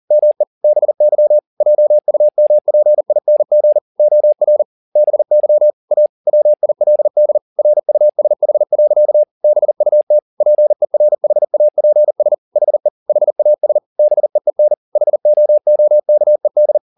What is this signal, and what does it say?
Morse code, heard as a keyed tone, about 40 wpm, 600 Hz. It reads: ME BY JUMPING OR BY A WILD RUSH. BUT PERHAPS HE HAS BEEN HOOKED